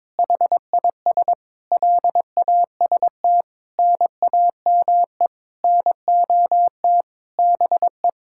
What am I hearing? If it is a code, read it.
Morse code, heard as a keyed tone, about 22 wpm, 700 Hz. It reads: HIS LAST NAME NOT BE
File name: Sentences with 9-Word Phrases - 22wpm